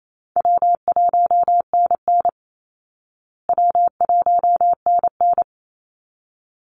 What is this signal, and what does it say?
Morse code, heard as a keyed tone, about 28 wpm, 700 Hz. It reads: W1NN W1NN
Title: Call Signs Repeated 3x - 28wpm